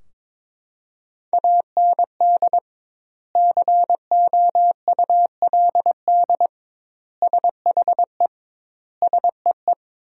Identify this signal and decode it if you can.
Morse code, heard as a keyed tone, about 22 wpm, 700 Hz. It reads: AND COULD SHE SEE
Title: 2x Word-Spacing - Sentences from Top 100 Words - 22wpm